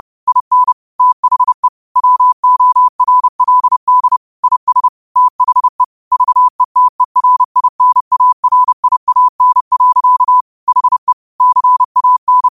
Morse code, heard as a keyed tone, about 30 wpm, 1000 Hz. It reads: IN THE WORLD IS THE VETERINARIAN. HE CAN
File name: qotdfiles / 30wpm